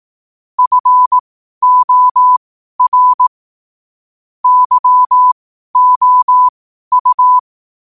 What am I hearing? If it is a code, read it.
Morse code, heard as a keyed tone, about 18 wpm, spaced as for 12 wpm, 1000 Hz. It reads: FOR YOU
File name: qotdfiles / 12wpm